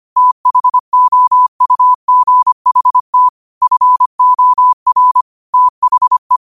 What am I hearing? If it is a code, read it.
Morse code, heard as a keyed tone, about 25 wpm, 1000 Hz. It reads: THOUGHT FOR THE